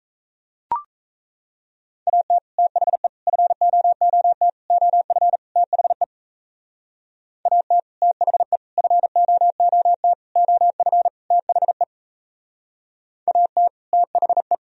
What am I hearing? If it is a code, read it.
Morse code, heard as a keyed tone, about 36 wpm, 700 Hz. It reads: AT THE FOOT OF THE AT THE FOOT OF THE AT THE